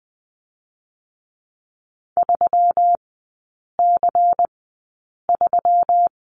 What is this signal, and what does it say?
Morse code, heard as a keyed tone, about 19 wpm, spaced as for 8 wpm, 700 Hz. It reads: 3C3
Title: 121 Lesson 20e - Number 3 - Three Characters 20wpm